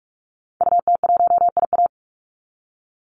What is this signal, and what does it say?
Morse code, heard as a keyed tone, about 44 wpm, 700 Hz. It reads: UT1IA
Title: Call Signs - 45wpm